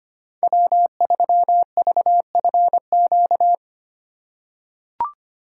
Morse code, heard as a keyed tone, about 25 wpm, 700 Hz. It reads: W3VFQ
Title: USA Call Signs - 5 Characters - 3x Character Spacing - 25wpm